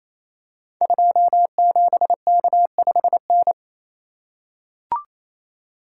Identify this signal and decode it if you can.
Morse code, heard as a keyed tone, about 27 wpm, 700 Hz. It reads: 27K5N
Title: Code Groups - 5 Letter:Numbers - 28wpm